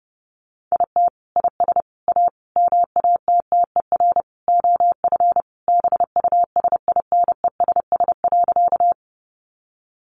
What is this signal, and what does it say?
Morse code, heard as a keyed tone, about 30 wpm, 700 Hz. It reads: IT IS A MATTER OF BUSINESS.
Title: Sentences from Top 700 Words 30wpm